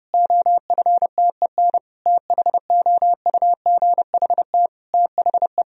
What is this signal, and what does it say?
Morse code, heard as a keyed tone, about 30 wpm, 700 Hz. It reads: OFTEN THOUGHT THE
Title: Sentences from Top 1000 Words - Review Part 1 - 30wpm